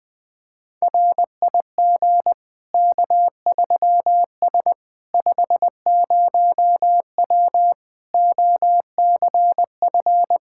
Morse code, heard as a keyed tone, about 20 wpm, 700 Hz. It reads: RIG K3S 50W OCF
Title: K1USN SST QSO Practice 20wpm